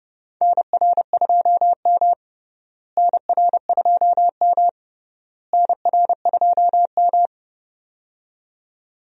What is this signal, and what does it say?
Morse code, heard as a keyed tone, about 30 wpm, 700 Hz. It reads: NR2M NR2M NR2M
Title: USA Call Signs - 4 Characters - Repeated 3x - 30wpm